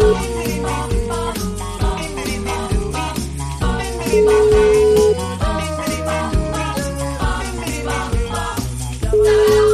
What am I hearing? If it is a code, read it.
Ringback tone (Europe, 425 Hz), with noise.